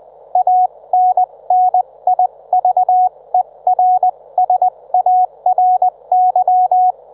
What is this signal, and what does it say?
Morse code, heard as a keyed tone, about 20 wpm, spaced as for 17 wpm, 730 Hz. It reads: ANNIVERSARY
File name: Trivia